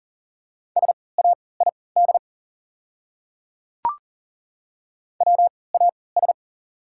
Morse code, heard as a keyed tone, about 40 wpm, spaced as for 23 wpm, 690 Hz. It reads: SAID WAS
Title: 3x Character Spacing - Top 100 Words - 40wpm